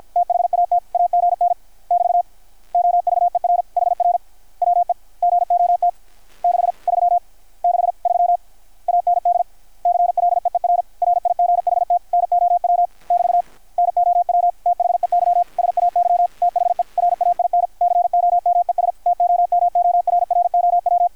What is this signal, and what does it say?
Morse code, heard as a keyed tone, about 53 wpm, 720 Hz. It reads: THAT AGN <BT> OVER HR WE GOT 6 3 6 3 AND CLEER RIGHT NOW <BT> NOW THEY SAY THE RAIN COMES TOMORROW
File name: archived / CRSnet